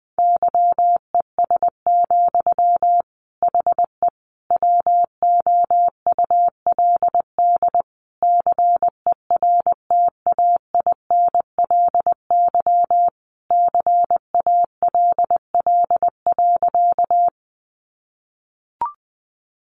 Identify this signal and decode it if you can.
Morse code, heard as a keyed tone, about 20 wpm, 700 Hz. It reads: YES, HE WOULD CERTAINLY CALL.
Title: Sentences from Top 900 Words 20wpm